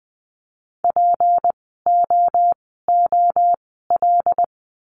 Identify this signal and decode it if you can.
Morse code, heard as a keyed tone, about 20 wpm, spaced as for 15 wpm, 700 Hz. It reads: POOL
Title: 153 Lesson 25f - Letter p - Words 20wpm